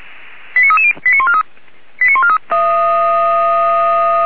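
Five-tone selective call (ZVEI), heard as quick runs of five tones, tones of about 80 ms. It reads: 89330 89143 89143
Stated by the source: ZVEI five-tone call received by SDR, Feuerwehr Orlamünde.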